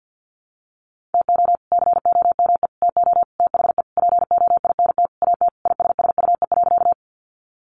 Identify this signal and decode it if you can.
Morse code, heard as a keyed tone, about 49 wpm, 700 Hz. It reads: TO COME TO THE POINT AT ISSUE.
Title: Sentences from Top 900 Words 50wpm